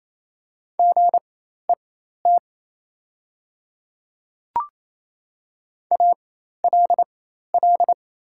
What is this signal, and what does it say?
Morse code, heard as a keyed tone, about 28 wpm, spaced as for 13 wpm, 700 Hz. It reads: GET ALL
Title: 4x Character Spacing - Top 200 Words - 28wpm